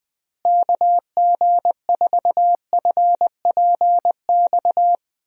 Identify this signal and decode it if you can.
Morse code, heard as a keyed tone, about 20 wpm, 700 Hz. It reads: KG4FPX
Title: USA Call Signs - 6 Characters - Repeated 3x - 20wpm